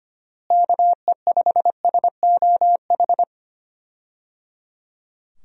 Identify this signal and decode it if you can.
Morse code, heard as a keyed tone, about 25 wpm, 700 Hz. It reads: KE5SOH
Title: USA Call Signs - 6 Characters - Repeated 3x - 25wpm